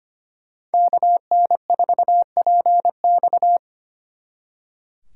Morse code, heard as a keyed tone, about 33 wpm, 700 Hz. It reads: KN4PX